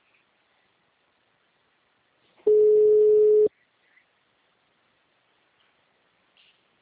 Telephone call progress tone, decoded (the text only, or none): ringback tone